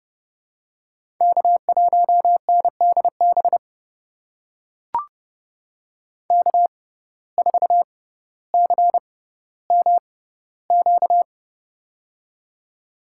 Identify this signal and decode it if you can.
Morse code, heard as a keyed tone, about 30 wpm, spaced as for 15 wpm, 700 Hz. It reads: K1NDB K4CMQ
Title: USA Call Signs - 5 Characters - 6x Character Spacing - 30wpm